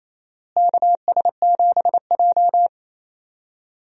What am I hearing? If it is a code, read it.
Morse code, heard as a keyed tone, about 28 wpm, 700 Hz. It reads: KS7J
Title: USA Call Signs - 4 Characters - Repeated 2x - 28wpm